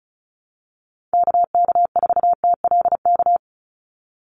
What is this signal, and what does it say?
Morse code, heard as a keyed tone, about 34 wpm, 700 Hz. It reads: KK4TLK
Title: USA Call Signs 35wpm